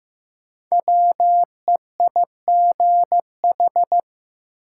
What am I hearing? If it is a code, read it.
Morse code, heard as a keyed tone, about 16 wpm, 700 Hz. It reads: WEIGH
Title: Five Letter Words - 15wpm